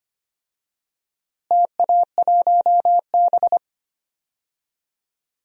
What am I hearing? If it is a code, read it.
Morse code, heard as a keyed tone, about 25 wpm, 700 Hz. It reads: TA1B
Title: DXCC-100 Call Signs - 4 Characters - Repeated 3x - 25wpm